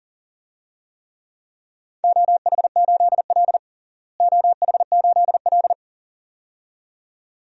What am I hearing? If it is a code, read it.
Morse code, heard as a keyed tone, about 40 wpm, 700 Hz. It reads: OH8L OH8L
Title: DXCC-100 Call Signs - 4 Characters - Repeated 2x - 40wpm